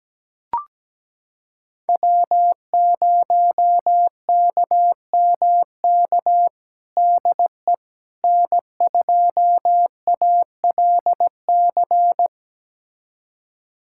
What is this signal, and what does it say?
Morse code, heard as a keyed tone, about 17 wpm, 700 Hz. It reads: W0KMK DE N2ALC
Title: USA Callsign DE Callsign - 5 Characters - 17wpm